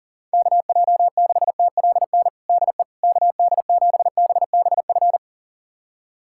Morse code, heard as a keyed tone, about 40 wpm, 700 Hz. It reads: KJ6TLN DE KD7BBF
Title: USA Callsign DE Callsign - 6 Characters - 40wpm